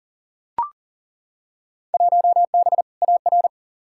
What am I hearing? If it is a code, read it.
Morse code, heard as a keyed tone, about 40 wpm, 700 Hz. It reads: 1B AR
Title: Field Day Exchanges Encore - 40wpm